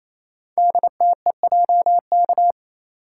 Morse code, heard as a keyed tone, about 29 wpm, 700 Hz. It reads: DTEJK
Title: Code Groups - 5 Letters - 28wpm